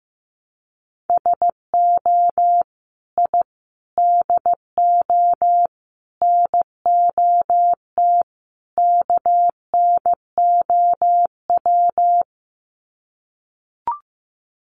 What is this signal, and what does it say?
Morse code, heard as a keyed tone, about 15 wpm, 700 Hz. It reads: SO I DO NOT KNOW
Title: Sets of 5 Words Encore 15wpm